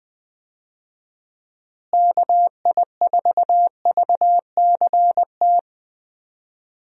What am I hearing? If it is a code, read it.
Morse code, heard as a keyed tone, about 20 wpm, 700 Hz. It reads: KI4VCT
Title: USA Call Signs - 6 Characters - Repeated 3x - 20wpm